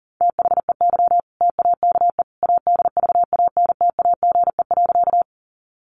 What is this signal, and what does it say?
Morse code, heard as a keyed tone, about 40 wpm, 700 Hz. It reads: THEY TAKE ADVANTAGE.
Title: Sentences from Top 900 Words 40wpm